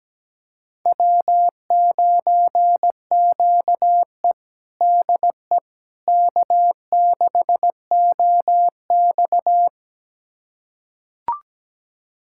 Morse code, heard as a keyed tone, about 17 wpm, 700 Hz. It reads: W9QE DE K6OX
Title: USA Callsign DE Callsign - 4 Characters - 17wpm